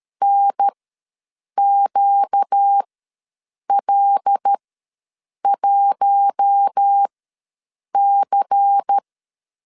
Morse code, heard as a keyed tone, about 13 wpm, spaced as for 7 wpm, 800 Hz. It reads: NQL1C
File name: LEZIONI / 9 2 ws 12wpm 3406